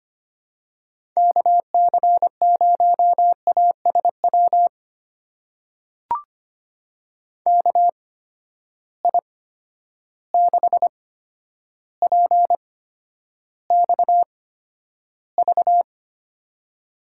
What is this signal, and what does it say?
Morse code, heard as a keyed tone, about 25 wpm, spaced as for 11 wpm, 700 Hz. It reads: KC0ASW KI6PXV
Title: USA Call Signs - 6 Characters - 8x Character Spacing - 25wpm